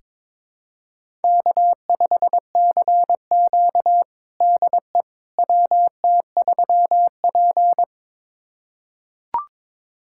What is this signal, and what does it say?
Morse code, heard as a keyed tone, about 22 wpm, 700 Hz. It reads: K5CQ DE WT3P
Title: USA Callsign DE Callsign - 4 Characters - 22wpm